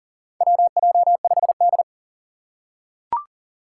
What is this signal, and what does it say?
Morse code, heard as a keyed tone, about 42 wpm, 690 Hz. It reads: WJ5D
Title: USA Call Signs - 4 Characters - 2x Character Spacing - 40wpm